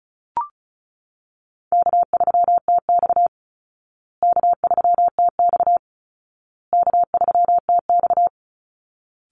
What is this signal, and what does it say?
Morse code, heard as a keyed tone, about 34 wpm, 700 Hz. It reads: K3TX K3TX K3TX
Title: Easy Call Signs Repeated 3x - 35wpm